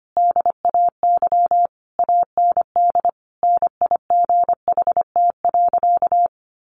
Morse code, heard as a keyed tone, about 25 wpm, 700 Hz. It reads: DAY AND NIGHT.
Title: Sentences from Top 1500 Words 25wpm